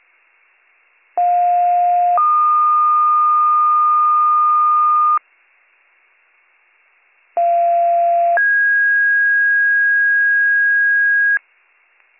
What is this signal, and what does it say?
Two-tone page, heard as a one-second tone, then a three-second tone, about 710 then 1220 Hz, about 690 then 1650 Hz. It is heard from 1.2 to 11.4 s.